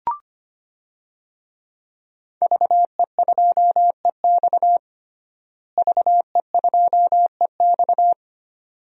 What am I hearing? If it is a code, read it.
Morse code, heard as a keyed tone, about 25 wpm, 700 Hz. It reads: VE2EX VE2EX
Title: DXCC-100 Call Signs - 5 Characters - Repeated 2x - 25wpm